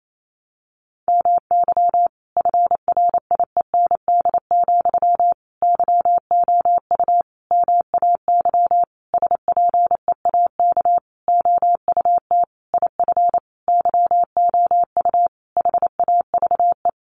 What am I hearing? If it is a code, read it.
Morse code, heard as a keyed tone, about 28 wpm, 700 Hz. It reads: MY FRIEND, YOU MAY SPEAK OUT IF YOU HAVE
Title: Sentences from Top 1300 Words 28wpm